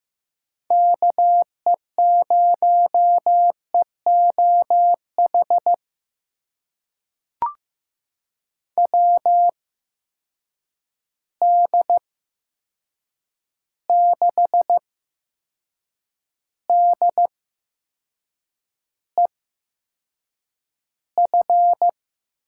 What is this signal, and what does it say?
Morse code, heard as a keyed tone, about 15 wpm, spaced as for 6 wpm, 700 Hz. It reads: KE0EOH WD6DEF
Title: USA Call Signs - 6 Characters - 8x Character Spacing - 15wpm